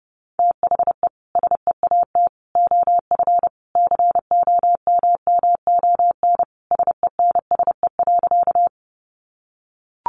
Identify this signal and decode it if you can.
Morse code, heard as a keyed tone, about 30 wpm, 700 Hz. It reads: THE SEAT OF COMMON SENSE.
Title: Sentences from Top 1500 Words 30wpm